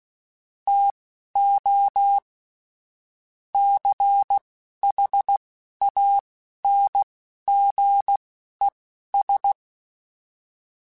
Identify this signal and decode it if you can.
Morse code, heard as a keyed tone, about 16 wpm, spaced as for 11 wpm, 800 Hz. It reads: TO CHANGES
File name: Technical Subjects / Technical Subjects-mp3s / Reactance @10